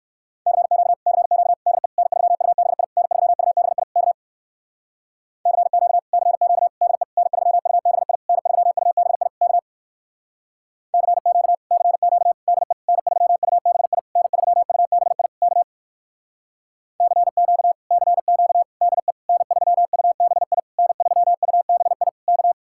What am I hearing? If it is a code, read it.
Morse code, heard as a keyed tone, about 57 wpm, 700 Hz. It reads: CQ CQ DE N3UBI N3UBI K CQ CQ DE N3UBI N3UBI K CQ CQ DE N3UBI N3UBI K CQ CQ DE N3UBI N3UBI K